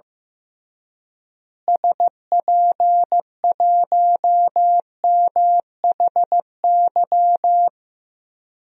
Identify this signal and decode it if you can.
Morse code, heard as a keyed tone, about 15 wpm, 700 Hz. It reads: SP1MHY
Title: DXCC-100 Call Signs - 6 Characters - Repeated 3x - 15wpm